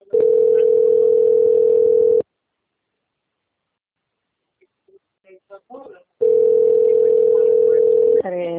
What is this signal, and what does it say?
Ringback tone (North America), 440 + 480 Hz, with speech.
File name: out- - - - - .32860